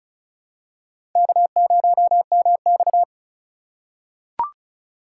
Morse code, heard as a keyed tone, about 36 wpm, 690 Hz. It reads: K0MX
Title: USA Call Signs - 4 Characters - 3x Character Spacing - 35wpm